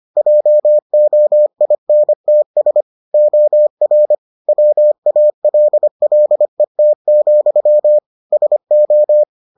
Morse code, heard as a keyed tone, about 25 wpm, 600 Hz. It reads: JOINTS OR WALLET, SO